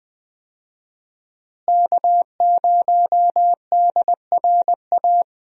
Morse code, heard as a keyed tone, about 20 wpm, 700 Hz. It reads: K0DRA
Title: USA Call Signs - 5 Characters - Repeated 2x - 20wpm